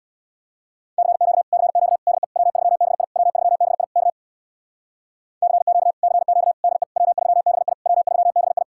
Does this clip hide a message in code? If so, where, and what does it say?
Yes: Morse code from 1.0 to 8.6 s, heard as a keyed tone, about 66 wpm, 700 Hz. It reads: CQ CQ DE W3BI W3BI K CQ CQ DE W3BI W3BI